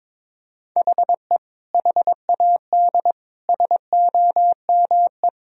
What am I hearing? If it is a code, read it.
Morse code, heard as a keyed tone, about 21 wpm, 700 Hz. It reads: HE HAD SOME
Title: Sentences from Top 1000 Words - Review Part 3 - 22wpm